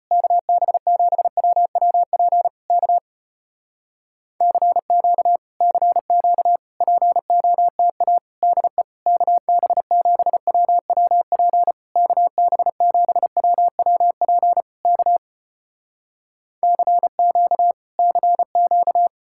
Morse code, heard as a keyed tone, about 35 wpm, 700 Hz. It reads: KB7WWP K CQ CQ POTA DE KB7WWP KB7WWP K CQ CQ